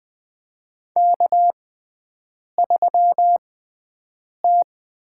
Morse code, heard as a keyed tone, about 20 wpm, spaced as for 7 wpm, 700 Hz. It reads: K3T
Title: USA Call Signs - 6x Character Spacing - 20wpm